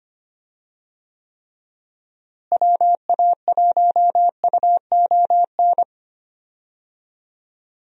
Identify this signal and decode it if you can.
Morse code, heard as a keyed tone, about 25 wpm, 700 Hz. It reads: WA1UON
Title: USA Call Signs - 6 Characters - Repeated 2x - 25wpm